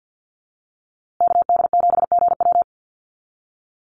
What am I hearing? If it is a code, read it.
Morse code, heard as a keyed tone, about 50 wpm, 700 Hz. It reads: KD7GW
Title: USA Call Signs Repeated 2x 50wpm